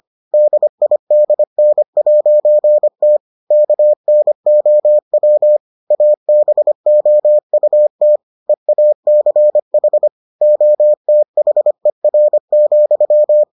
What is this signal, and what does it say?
Morse code, heard as a keyed tone, about 25 wpm, 600 Hz. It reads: DIDN'T KNOW ABOUT EACH OTHER,